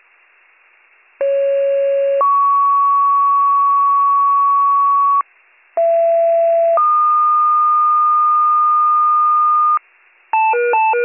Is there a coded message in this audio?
Yes: two-tone page from 1.2 to 9.8 s, heard as a one-second tone, then a three-second tone, about 550 then 1120 Hz, about 670 then 1220 Hz.